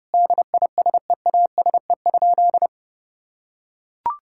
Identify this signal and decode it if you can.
Morse code, heard as a keyed tone, about 30 wpm, 700 Hz. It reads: DISEASE?
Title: Sentences from Top 1000 Words - Review Part 6 - 30wpm